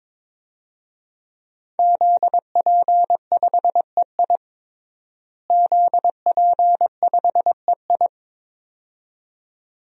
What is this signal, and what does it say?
Morse code, heard as a keyed tone, about 22 wpm, 700 Hz. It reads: ZP5EI ZP5EI
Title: DXCC-100 Call Signs - 5 Characters - Repeated 2x - 22wpm